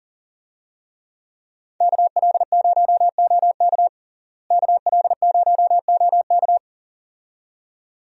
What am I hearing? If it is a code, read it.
Morse code, heard as a keyed tone, about 37 wpm, 700 Hz. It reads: KL0OK KL0OK
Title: USA Call Signs - 5 Characters - Repeated 2x - 40wpm